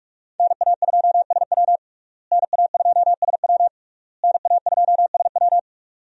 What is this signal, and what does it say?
Morse code, heard as a keyed tone, about 41 wpm, 700 Hz. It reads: NA2SW NA2SW NA2SW
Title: USA Call Signs - 5 Characters - Repeated 3x - 45wpm